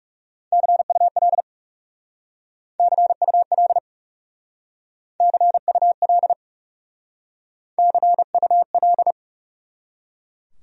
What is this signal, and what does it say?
Morse code, heard as a keyed tone, about 37 wpm, 700 Hz. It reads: CUL CUL CUL CUL
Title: Speed-Racing Abbreviations - 30wpm